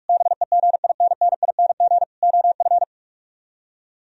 Morse code, heard as a keyed tone, about 44 wpm, 700 Hz. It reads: BEGINNING OF